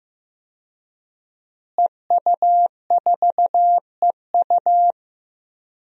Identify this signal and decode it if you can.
Morse code, heard as a keyed tone, about 15 wpm, 700 Hz. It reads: EU4EU
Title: DXCC-100 Call Signs - 5 Characters - Repeated 2x - 15wpm